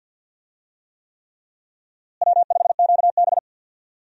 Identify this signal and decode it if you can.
Morse code, heard as a keyed tone, about 46 wpm, 700 Hz. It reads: W5QB
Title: USA Call Signs - 4 Characters - Repeated 3x - 50wpm